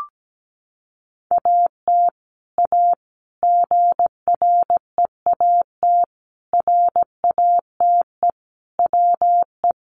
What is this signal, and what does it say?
Morse code, heard as a keyed tone, about 17 wpm, 700 Hz. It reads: AT A GREAT RATE WE
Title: Sentences from Top 1200 Words 17wpm